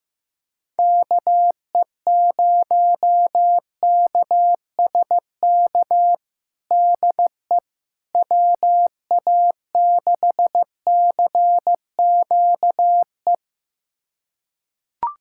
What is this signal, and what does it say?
Morse code, heard as a keyed tone, about 15 wpm, 700 Hz. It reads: KE0KSK DE WA6CQE
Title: USA Callsign DE Callsign 15wpm